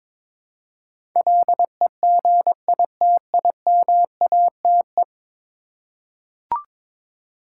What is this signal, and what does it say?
Morse code, heard as a keyed tone, about 22 wpm, 700 Hz. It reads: LEGITIMATE
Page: morse-code-ninja-practice-22wpm